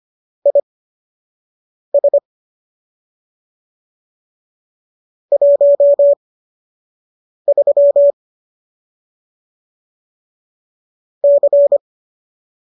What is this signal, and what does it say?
Morse code, heard as a keyed tone, about 25 wpm, spaced as for 6 wpm, 570 Hz. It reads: IS 13 C